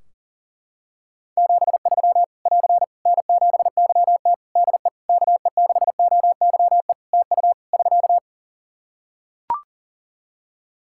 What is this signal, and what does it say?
Morse code, heard as a keyed tone, about 40 wpm, 700 Hz. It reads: 73 <AR> N7YT DE KE6OYE TU <SK>
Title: Practice QSO - Exchange 4 - Ending - 40wpm